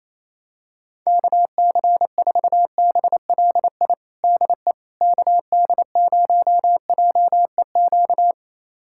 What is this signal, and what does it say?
Morse code, heard as a keyed tone, about 28 wpm, 700 Hz. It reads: KC4BLI DE KD0JEQ
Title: USA Callsign DE Callsign 28wpm